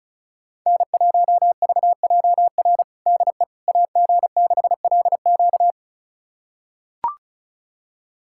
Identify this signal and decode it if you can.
Morse code, heard as a keyed tone, about 36 wpm, 700 Hz. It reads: N1VJR DE AG6LQ
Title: USA Callsign DE Callsign - 5 Characters - 35wpm